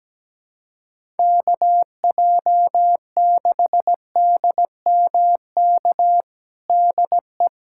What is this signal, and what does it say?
Morse code, heard as a keyed tone, about 17 wpm, 700 Hz. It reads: KJ6DMK DE
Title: USA Callsign DE Callsign - 6 Characters - 17wpm